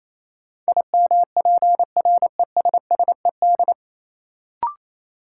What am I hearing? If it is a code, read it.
Morse code, heard as a keyed tone, about 28 wpm, 700 Hz. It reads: IMPRESSED
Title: Sentences from Top 1600 Words - New Word Drill - 28wpm